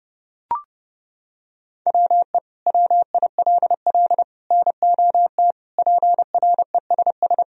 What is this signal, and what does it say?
Morse code, heard as a keyed tone, about 31 wpm, 700 Hz. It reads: WE WILL NOT PRESS
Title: Sentences from Top 1000 Words - Review Part 1 - 30wpm